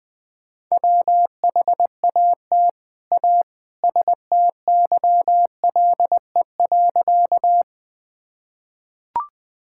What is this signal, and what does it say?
Morse code, heard as a keyed tone, about 21 wpm, 700 Hz. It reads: WHAT A STYLE.
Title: Sentences from Top 1000 Words - Review Part 5 - 20wpm